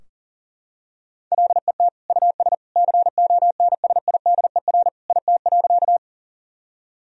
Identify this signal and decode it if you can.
Morse code, heard as a keyed tone, about 40 wpm, 700 Hz. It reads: LET US CONSIDER IT.
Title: Sentences from Top 1000 Words - Review Part 4 - 40wpm